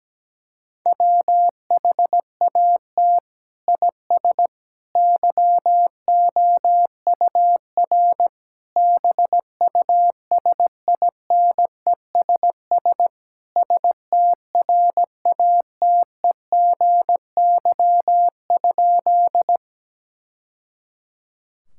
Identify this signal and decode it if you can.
Morse code, heard as a keyed tone, about 17 wpm, 700 Hz. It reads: WHAT IS YOUR BUSINESS STRATEGY?
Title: Sentences from Top 1000 Words - Review Part 5 - 17wpm